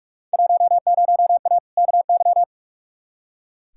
Morse code, heard as a keyed tone, about 45 wpm, 700 Hz. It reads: 10A KY
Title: Field Day Exchanges Encore - 45wpm